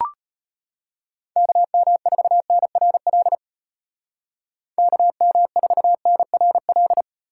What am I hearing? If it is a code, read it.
Morse code, heard as a keyed tone, about 34 wpm, 700 Hz. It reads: KM4NRL KM4NRL